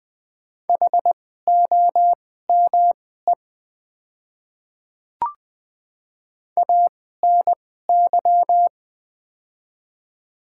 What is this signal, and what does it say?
Morse code, heard as a keyed tone, about 20 wpm, spaced as for 14 wpm, 700 Hz. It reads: HOME ANY
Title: 2x Character Spacing - Top 200 Words - 20wpm